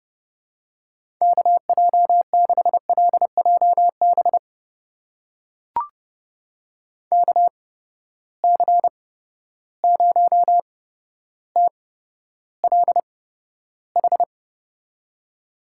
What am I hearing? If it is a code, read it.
Morse code, heard as a keyed tone, about 29 wpm, spaced as for 13 wpm, 700 Hz. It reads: KJ6LJB KC0TLH
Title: USA Call Signs - 6 Characters - 8x Character Spacing - 30wpm